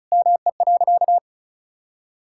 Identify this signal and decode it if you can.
Morse code, heard as a keyed tone, about 36 wpm, 700 Hz. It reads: ME.
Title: Sentences from Top 1600 Words 35wpm